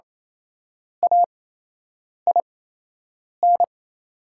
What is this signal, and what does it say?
Morse code, heard as a keyed tone, about 28 wpm, spaced as for 8 wpm, 700 Hz. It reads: AIN
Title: 8x Character Spacing - Top 500 Words - 28wpm